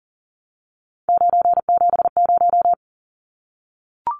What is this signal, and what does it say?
Morse code, heard as a keyed tone, about 40 wpm, 700 Hz. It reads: 970
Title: Numbers - 3-Digit 40wpm